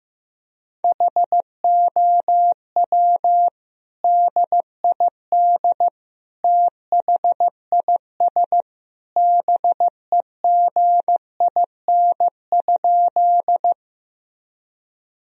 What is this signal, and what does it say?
Morse code, heard as a keyed tone, about 16 wpm, 700 Hz. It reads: HOW DID THIS BEGIN?
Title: Sentences from Top 1000 Words - Review Part 1 - 15wpm